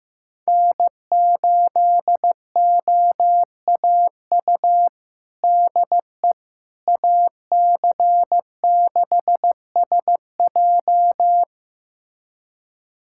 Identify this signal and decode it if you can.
Morse code, heard as a keyed tone, about 15 wpm, 700 Hz. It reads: N8OAU DE AC6SJ
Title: USA Callsign DE Callsign - 5 Characters - 15wpm